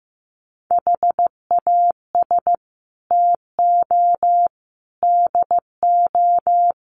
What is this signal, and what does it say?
Morse code, heard as a keyed tone, about 15 wpm, 700 Hz. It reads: HAS TO DO
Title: Sets of 4 Words 15wpm